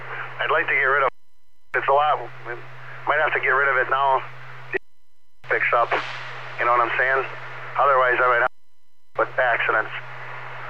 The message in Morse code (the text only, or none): none